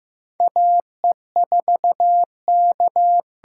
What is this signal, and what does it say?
Morse code, heard as a keyed tone, about 15 wpm, 700 Hz. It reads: A E 4 K
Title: USA Call Signs - 6x Character Spacing - 15wpm